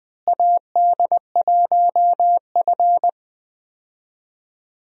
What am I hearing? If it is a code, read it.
Morse code, heard as a keyed tone, about 21 wpm, 700 Hz. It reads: AD1F
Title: USA Call Signs - 4 Characters - Repeated 3x - 20wpm